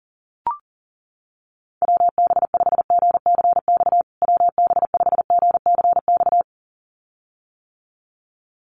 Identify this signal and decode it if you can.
Morse code, heard as a keyed tone, about 40 wpm, 700 Hz. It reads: WB5GCX WB5GCX
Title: USA Call Signs Repeated 2x 40wpm